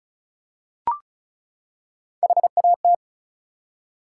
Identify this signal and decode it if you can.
Morse code, heard as a keyed tone, about 36 wpm, 700 Hz. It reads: HAT
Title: Three Letter Words - 35wpm